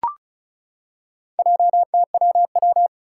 Morse code, heard as a keyed tone, about 34 wpm, 700 Hz. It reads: JTWW
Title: Four Challenging Letter-Number Combinations 35wpm